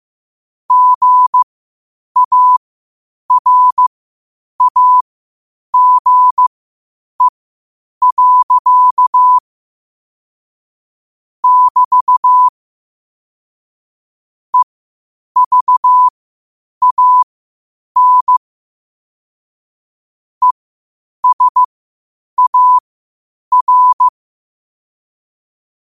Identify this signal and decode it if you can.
Morse code, heard as a keyed tone, about 15 wpm, spaced as for 8 wpm, 1000 Hz. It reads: GARAGE. <BT> EVAN ESAR
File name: qotdfiles / 8wpm